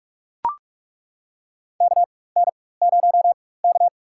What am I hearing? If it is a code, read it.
Morse code, heard as a keyed tone, about 41 wpm, spaced as for 20 wpm, 700 Hz. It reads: KN0K
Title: USA Call Signs - 4 Characters - 4x Character Spacing - 45wpm